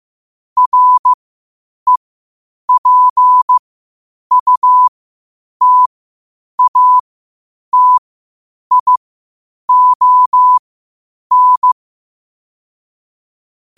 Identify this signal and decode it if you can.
Morse code, heard as a keyed tone, about 15 wpm, spaced as for 8 wpm, 1000 Hz. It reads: REPUTATION
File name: qotdfiles / 8wpm